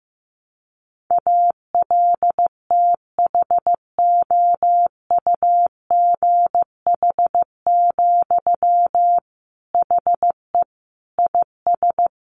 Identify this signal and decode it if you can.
Morse code, heard as a keyed tone, about 15 wpm, 700 Hz. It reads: ALTHOUGH, HE IS
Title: Sentences from Top 600 Words 15wpm